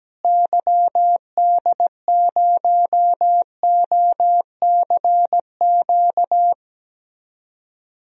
Morse code, heard as a keyed tone, about 17 wpm, 700 Hz. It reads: YD0OCQ
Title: DXCC-100 Call Signs - 6 Characters - Repeated 3x - 17wpm